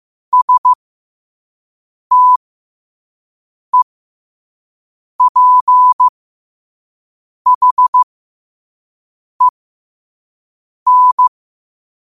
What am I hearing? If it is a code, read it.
Morse code, heard as a keyed tone, about 15 wpm, spaced as for 5 wpm, 1000 Hz. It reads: STEPHEN
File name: qotdfiles / 5wpm